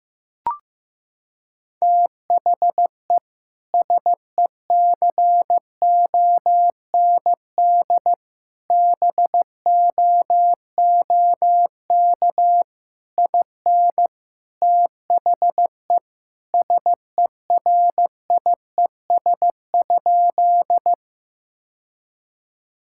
Morse code, heard as a keyed tone, about 15 wpm, 700 Hz. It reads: THE SECOND BOOK IN THE SERIES?
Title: Sentences from Top 1000 Words - Review Part 5 - 15wpm